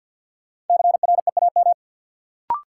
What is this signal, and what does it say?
Morse code, heard as a keyed tone, about 52 wpm, 700 Hz. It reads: CREAM
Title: Sentences from Top 1500 Words - New Word Drill - 50wpm